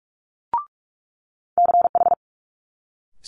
Morse code, heard as a keyed tone, about 46 wpm, 700 Hz. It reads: CH
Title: Multi-letter Phonemes Quick Fire 45wpm